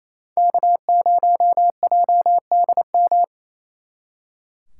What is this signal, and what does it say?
Morse code, heard as a keyed tone, about 35 wpm, 700 Hz. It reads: K0JDM
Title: Speed Racing - USA 1x3 Call Signs 28wpm